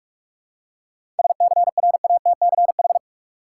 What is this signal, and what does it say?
Morse code, heard as a keyed tone, about 47 wpm, 700 Hz. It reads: SCRATCH
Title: Seven Letter Words - 45wpm